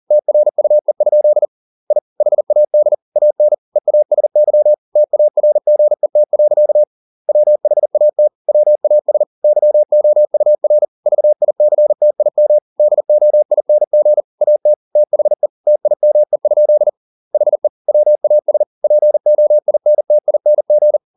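Morse code, heard as a keyed tone, about 40 wpm, 600 Hz. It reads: TRUE? I HAD AN EASY TARGET. WHAT WAS YOUR VICTIM DOING AT THE TIME? HE WAS POINTING
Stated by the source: Morse code at 40 wpm, practice text, CW med Gnister.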